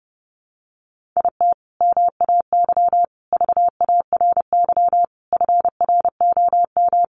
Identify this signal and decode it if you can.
Morse code, heard as a keyed tone, about 28 wpm, 700 Hz. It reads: IT MAY VARY FROM
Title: Sentences from Top 1200 Words 30wpm